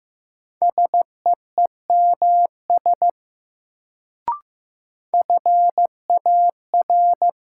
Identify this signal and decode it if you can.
Morse code, heard as a keyed tone, about 15 wpm, 700 Hz. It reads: SEEMS FAR
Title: Sets of 3 Words - New Word Drill - 15wpm